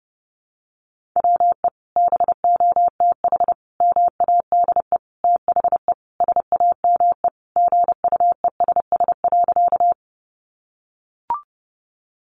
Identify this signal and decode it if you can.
Morse code, heard as a keyed tone, about 30 wpm, 700 Hz. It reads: WE BOTH MADE THE SAME GUESS.
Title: Sentences from Top 1500 Words 30wpm